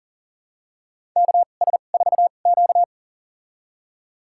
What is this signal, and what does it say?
Morse code, heard as a keyed tone, about 40 wpm, spaced as for 29 wpm, 700 Hz. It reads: KS4Q
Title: USA Call Signs - 4 Characters - 2x Character Spacing - 40wpm